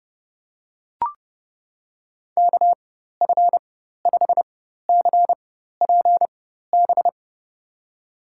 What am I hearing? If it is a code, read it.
Morse code, heard as a keyed tone, about 30 wpm, spaced as for 14 wpm, 700 Hz. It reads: KF5CPB
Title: USA Call Signs - 6 Characters - 4x Character Spacing - 30wpm